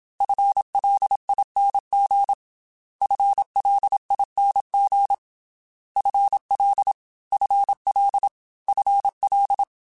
Morse code, heard as a keyed tone, about 27 wpm, 800 Hz. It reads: FLING FLING FL FL FL
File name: FL / FL mp3s / FL Prefixes 3x2 Twice @25